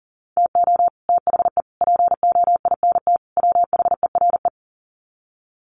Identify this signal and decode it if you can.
Morse code, heard as a keyed tone, about 40 wpm, 700 Hz. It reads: TO THE POINT WHERE
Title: Sets of 4 Words 40wpm